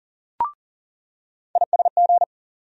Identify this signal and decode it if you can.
Morse code, heard as a keyed tone, about 40 wpm, 700 Hz. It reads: ISG
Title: USA Call Sign Postfixes - 3 Letters 40wpm